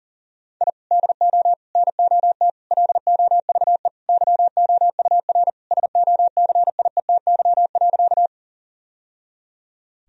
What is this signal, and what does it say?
Morse code, heard as a keyed tone, about 40 wpm, 700 Hz. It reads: I DO NOT LOVE YOUR SOCIETY.
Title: Sentences from Top 1000 Words - Review Part 4 - 40wpm